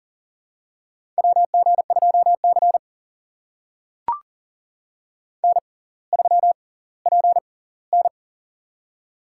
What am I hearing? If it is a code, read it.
Morse code, heard as a keyed tone, about 40 wpm, spaced as for 20 wpm, 700 Hz. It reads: WG2C N3PN
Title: USA Call Signs - 4 Characters - 6x Character Spacing - 40wpm